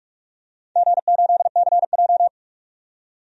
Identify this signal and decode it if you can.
Morse code, heard as a keyed tone, about 46 wpm, 700 Hz. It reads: G8CJ
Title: DXCC-100 Call Signs - 4 Characters - 45wpm